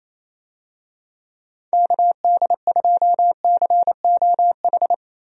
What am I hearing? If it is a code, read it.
Morse code, heard as a keyed tone, about 28 wpm, 700 Hz. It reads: KD2COH
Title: USA Call Signs - 6 Characters - 28wpm